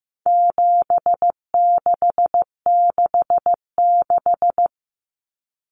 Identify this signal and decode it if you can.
Morse code, heard as a keyed tone, about 15 wpm, 700 Hz. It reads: 7666
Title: Numbers - 4-Digit 15wpm